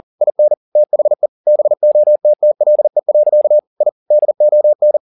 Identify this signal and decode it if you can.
Morse code, heard as a keyed tone, about 40 wpm, 600 Hz. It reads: IN THE BOTTLE. I DON